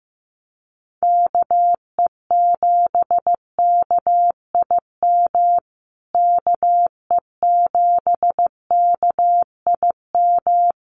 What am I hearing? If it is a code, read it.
Morse code, heard as a keyed tone, about 15 wpm, 700 Hz. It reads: KE7KIM KE7KIM
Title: USA Call Signs Repeated 3x 15wpm